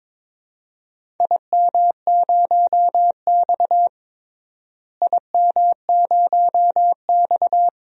Morse code, heard as a keyed tone, about 22 wpm, 700 Hz. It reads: IM0X IM0X
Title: DXCC-100 Call Signs - 4 Characters - Repeated 2x - 22wpm